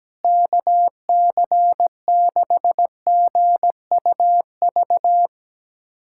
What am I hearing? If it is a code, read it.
Morse code, heard as a keyed tone, about 17 wpm, 700 Hz. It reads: KC6GUV
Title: USA Callsign DE Callsign 17wpm